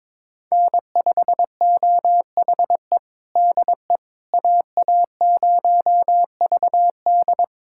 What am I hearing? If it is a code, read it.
Morse code, heard as a keyed tone, about 22 wpm, 700 Hz. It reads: N5OHE DE AA0VD
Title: USA Callsign DE Callsign - 5 Characters - 22wpm